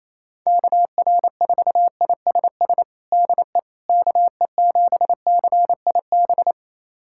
Morse code, heard as a keyed tone, about 28 wpm, 700 Hz. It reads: KR4ISS DE KE7CIB
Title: USA Callsign DE Callsign - 6 Characters - 28wpm